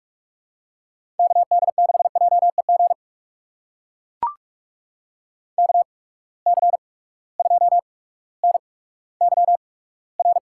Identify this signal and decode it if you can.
Morse code, heard as a keyed tone, about 42 wpm, spaced as for 19 wpm, 700 Hz. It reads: KD6JEG KC2NYR
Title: USA Call Signs - 6 Characters - 8x Character Spacing - 45wpm